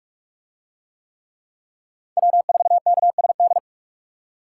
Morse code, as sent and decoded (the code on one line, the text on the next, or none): .-- ....- -.- ... -..
W4KSD